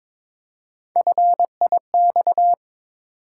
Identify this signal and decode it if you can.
Morse code, heard as a keyed tone, about 23 wpm, 700 Hz. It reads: FIX
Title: Three Letter Words - 22wpm